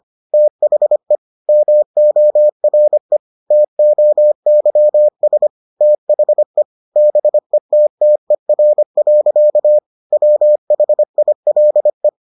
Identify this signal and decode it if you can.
Morse code, heard as a keyed tone, about 25 wpm, 600 Hz. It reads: THE MORE TOYS THE BETTER. WHILE